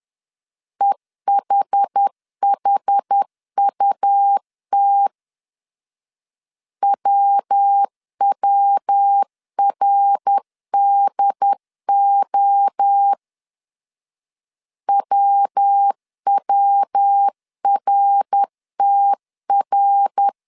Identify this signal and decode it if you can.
Morse code, heard as a keyed tone, about 10 wpm, 800 Hz. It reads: EHHUT WWRDO WWRTR
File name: CW / DET / 1 3 1500C S B 10wpm 3335